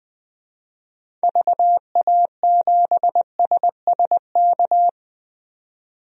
Morse code, heard as a keyed tone, about 20 wpm, 700 Hz. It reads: VA7SSK
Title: DXCC-100 Call Signs - 6 Characters - 20wpm